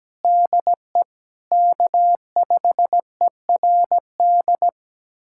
Morse code, heard as a keyed tone, about 17 wpm, 700 Hz. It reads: DE K5ERD
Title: USA Callsign DE Callsign - 5 Characters - 17wpm